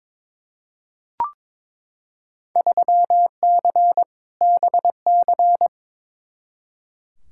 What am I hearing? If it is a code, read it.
Morse code, heard as a keyed tone, about 22 wpm, 700 Hz. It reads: 3C BC
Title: Field Day Exchanges Encore - 22wpm